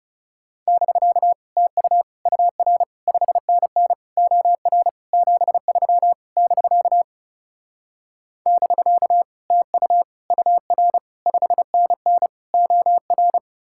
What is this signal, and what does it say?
Morse code, heard as a keyed tone, about 40 wpm, 700 Hz. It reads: <BK> TU UR 5NN OR 73 <BK> <BK> TU UR 5NN OR